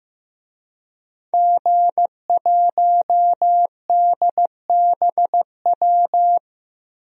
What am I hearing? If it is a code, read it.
Morse code, heard as a keyed tone, about 15 wpm, 700 Hz. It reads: G1DBW
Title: DXCC-100 Call Signs - 5 Characters - Repeated 3x - 15wpm